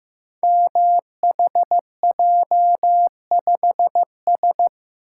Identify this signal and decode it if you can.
Morse code, heard as a keyed tone, about 15 wpm, 700 Hz. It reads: MHJ5S
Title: Code Groups - 5 Letter:Numbers - 15wpm